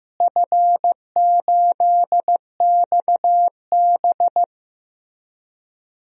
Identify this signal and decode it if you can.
Morse code, heard as a keyed tone, about 15 wpm, 700 Hz. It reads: F8XB
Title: DXCC-100 Call Signs - 4 Characters - Repeated 3x - 15wpm